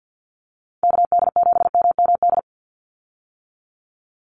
Morse code, heard as a keyed tone, about 50 wpm, 700 Hz. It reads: KD7MMD
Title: USA Call Signs 50wpm